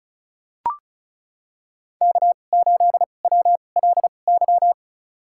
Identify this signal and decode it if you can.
Morse code, heard as a keyed tone, about 36 wpm, spaced as for 25 wpm, 700 Hz. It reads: K8WLY
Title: USA Call Signs - 5 Characters - 2x Character Spacing - 35wpm